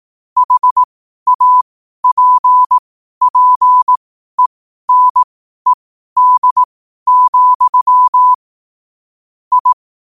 Morse code, heard as a keyed tone, about 18 wpm, spaced as for 12 wpm, 1000 Hz. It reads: HAPPENED, I
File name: qotdfiles / 12wpm